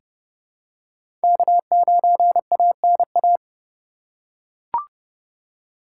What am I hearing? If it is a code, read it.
Morse code, heard as a keyed tone, about 30 wpm, 700 Hz. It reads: K9ANA
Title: USA Call Signs - 5 Characters - 3x Character Spacing - 30wpm